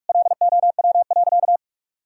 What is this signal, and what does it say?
Morse code, heard as a keyed tone, about 44 wpm, 700 Hz. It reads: LOW.
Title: Sentences from Top 1000 Words - Review Part 1 - 45wpm